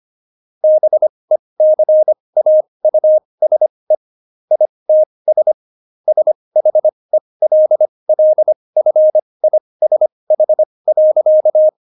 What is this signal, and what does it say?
Morse code, heard as a keyed tone, about 25 wpm, spaced as for 20 wpm, 620 Hz. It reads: BECAUSE ITS SHELLFISH.
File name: Jokes-25-char-20-eff